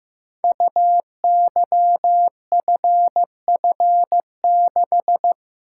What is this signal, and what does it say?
Morse code, heard as a keyed tone, about 15 wpm, 700 Hz. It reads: UYFF6
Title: Code Groups - 5 Letter:Numbers - 15wpm